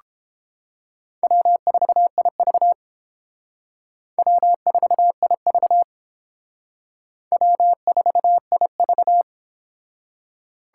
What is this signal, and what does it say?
Morse code, heard as a keyed tone, about 28 wpm, 700 Hz. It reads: W4IV W4IV W4IV